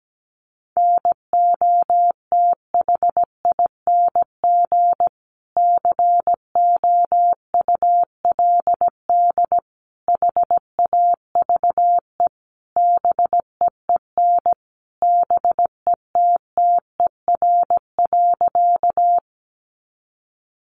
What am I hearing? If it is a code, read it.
Morse code, heard as a keyed tone, about 17 wpm, 700 Hz. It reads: NOTHING COULD HAVE BEEN BETTER.
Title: Sentences from Top 500 Words 17wpm